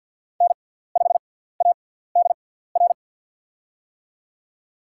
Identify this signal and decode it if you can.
Morse code, heard as a keyed tone, about 53 wpm, spaced as for 17 wpm, 700 Hz. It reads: N5ADR
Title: USA Call Signs - 5 Characters - 6x Character Spacing - 50wpm